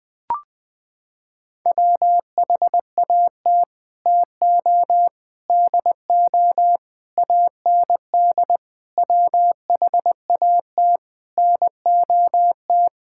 Morse code, heard as a keyed tone, about 20 wpm, 700 Hz. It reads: WHAT TO DO AND WHAT NOT
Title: Sets of 8 Words - 20wpm